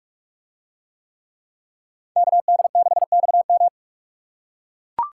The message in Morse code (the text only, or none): KD6XM